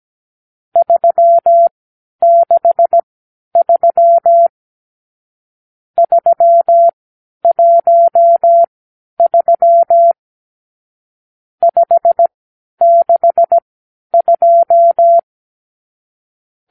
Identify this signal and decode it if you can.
Morse code, heard as a keyed tone, about 17 wpm, spaced as for 10 wpm, 680 Hz. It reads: 363 313 562
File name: k7qo / Lesson 50 (1-6)